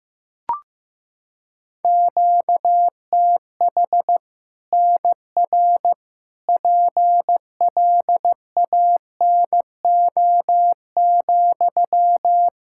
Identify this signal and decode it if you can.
Morse code, heard as a keyed tone, about 15 wpm, 700 Hz. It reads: QTH NR PLANO,
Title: Practice QSO - Exchange 1 - Beginning - 15wpm